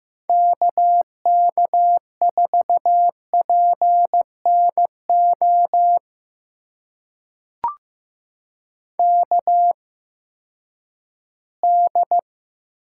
Morse code, heard as a keyed tone, about 15 wpm, spaced as for 10 wpm, 700 Hz. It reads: KK4PNO KD
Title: USA Call Signs - 6 Characters - 8x Character Spacing - 15wpm